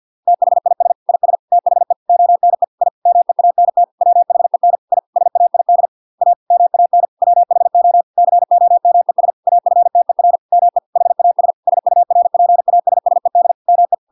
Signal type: Morse code